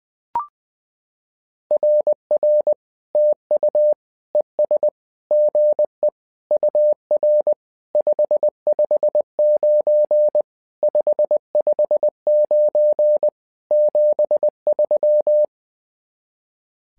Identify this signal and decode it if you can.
Morse code, heard as a keyed tone, about 20 wpm, 600 Hz. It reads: RR TU ES GE UR 559 559 73
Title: SOTA QSO Practice 20wpm